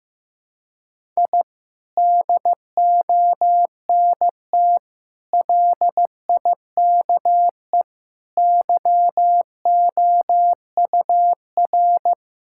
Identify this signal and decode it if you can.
Morse code, heard as a keyed tone, about 15 wpm, 700 Hz. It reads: I DONT LIKE YOUR
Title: Sentences from Top 1000 Words - Review Part 6 - 15wpm